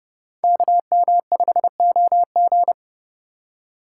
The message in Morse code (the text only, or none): KM5OG